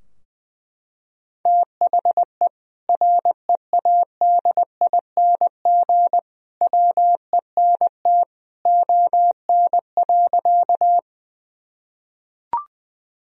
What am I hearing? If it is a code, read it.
Morse code, heard as a keyed tone, about 20 wpm, 700 Hz. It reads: THE READING WENT ON.
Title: Sentences from Top 1000 Words - Review Part 3 - 20wpm